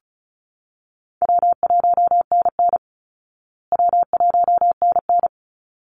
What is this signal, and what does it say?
Morse code, heard as a keyed tone, about 34 wpm, 700 Hz. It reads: W1NN W1NN
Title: Call Signs Repeated 3x - 35wpm